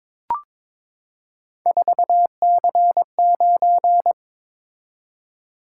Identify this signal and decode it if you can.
Morse code, heard as a keyed tone, about 23 wpm, 700 Hz. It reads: 4C9
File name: practice / 22wpm / Three Letter-Number Combinations - 22wpm